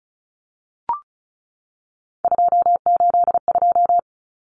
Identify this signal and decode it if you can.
Morse code, heard as a keyed tone, about 34 wpm, 700 Hz. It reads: 282
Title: Numbers - 3-Digit 35wpm